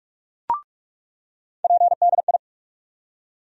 Morse code, heard as a keyed tone, about 46 wpm, 700 Hz. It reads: PDI
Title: USA Call Sign Postfixes - 3 Letters 45wpm